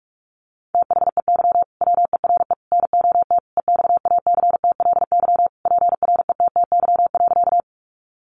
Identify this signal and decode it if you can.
Morse code, heard as a keyed tone, about 44 wpm, 700 Hz. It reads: THEY WERE NOT EXACTLY PRETTY.
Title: Sentences from Top 1000 Words 45wpm